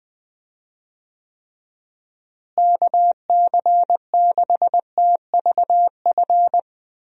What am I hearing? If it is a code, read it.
Morse code, heard as a keyed tone, about 20 wpm, 700 Hz. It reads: KC6TVF
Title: USA Call Signs - 6 Characters - Repeated 2x - 20wpm